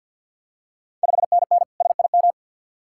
Morse code, heard as a keyed tone, about 52 wpm, 690 Hz. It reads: 5NN SIM
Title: Ontario QSO Party Exchange 50wpm